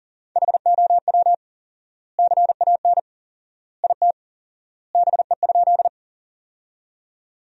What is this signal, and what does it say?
Morse code, heard as a keyed tone, about 37 wpm, 700 Hz. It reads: HOW CAN IT BE?
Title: 4x Word-Spacing - Sentences from Top 100 Words - 40wpm